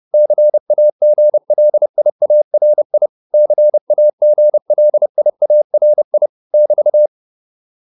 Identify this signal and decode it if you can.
Morse code, heard as a keyed tone, about 30 wpm, 600 Hz. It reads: CAGLIARI CAGLIARI <BT>